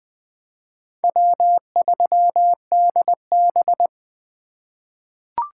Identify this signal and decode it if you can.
Morse code, heard as a keyed tone, about 20 wpm, 700 Hz. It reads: W3DB
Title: USA Call Signs - 4 Characters - 4x Character Spacing - 20wpm